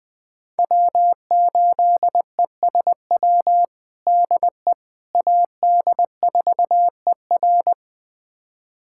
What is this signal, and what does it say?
Morse code, heard as a keyed tone, about 21 wpm, 700 Hz. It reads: W8ESW DE AD4ER
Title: USA Callsign DE Callsign - 5 Characters - 20wpm